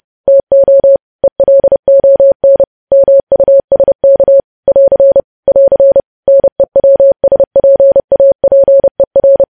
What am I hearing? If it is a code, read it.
Morse code, heard as a keyed tone, about 30 wpm, 550 Hz. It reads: TO ELON MUSK <AR> <AR> NEWSPAPER
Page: news headlines in morse code at 30 wpm